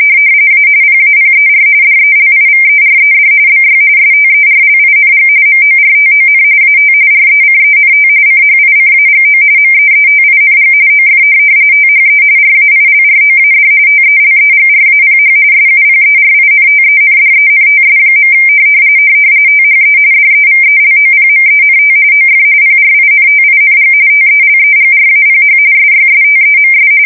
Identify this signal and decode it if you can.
RTTY, about 45.45 baud. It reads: A PAPER DELIVERED BY STERLING MORTON AT RECENT CHICAGO MEETING OF THE RAILROAD TELEGRAPH SUPERINTENDENTS. THE DEVELOPMENT OF PRINTING TELEGRAPH SYSTEMS
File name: fsk170 / printing-telegraph